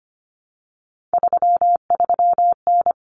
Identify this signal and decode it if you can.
Morse code, heard as a keyed tone, about 25 wpm, 700 Hz. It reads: 33N
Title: Common Cut Numbers - 3-Digit 25wpm